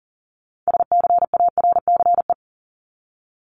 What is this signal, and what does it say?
Morse code, heard as a keyed tone, about 40 wpm, 700 Hz. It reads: SCARCE
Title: LY Suffix - Top 100 Words with Base Words 40wpm